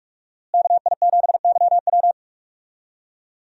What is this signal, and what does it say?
Morse code, heard as a keyed tone, about 44 wpm, 700 Hz. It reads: KI7YW
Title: USA Call Signs - 5 Characters - Repeated 3x - 45wpm